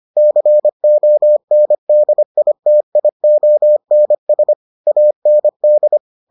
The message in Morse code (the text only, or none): CONDITIONS AND